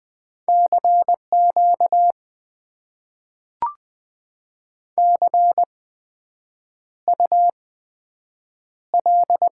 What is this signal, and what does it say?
Morse code, heard as a keyed tone, about 20 wpm, spaced as for 7 wpm, 700 Hz. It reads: CQ CUL
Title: Top 90 QSO Elements - 8x Character Spacing - 20wpm